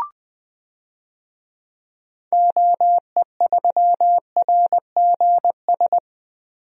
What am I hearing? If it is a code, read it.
Morse code, heard as a keyed tone, about 20 wpm, 700 Hz. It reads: OE3RGS
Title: DXCC-100 Call Signs - 6 Characters - 20wpm